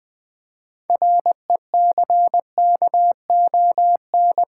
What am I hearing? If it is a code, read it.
Morse code, heard as a keyed tone, about 20 wpm, 700 Hz. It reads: RECKON
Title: 6 Letter Words - First 3 Letters Given - 20wpm